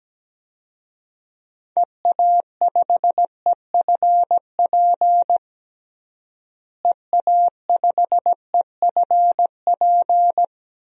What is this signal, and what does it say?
Morse code, heard as a keyed tone, about 17 wpm, 700 Hz. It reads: EA5EFP EA5EFP
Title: DXCC-100 Call Signs - 6 Characters - Repeated 2x - 17wpm